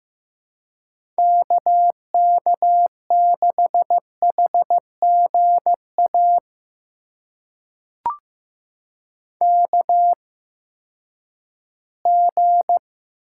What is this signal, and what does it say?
Morse code, heard as a keyed tone, about 15 wpm, spaced as for 10 wpm, 700 Hz. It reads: KK6HGA KG
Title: USA Call Signs - 6 Characters - 8x Character Spacing - 15wpm